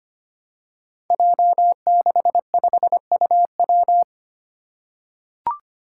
Morse code, heard as a keyed tone, about 25 wpm, 700 Hz. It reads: J65UW
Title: Five Challenging Letter-Number Combinations 25wpm